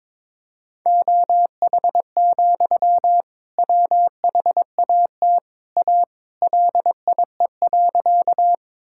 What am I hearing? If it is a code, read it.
Morse code, heard as a keyed tone, about 21 wpm, 700 Hz. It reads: OH, WHAT A LIE.
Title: Sentences from Top 1600 Words 22wpm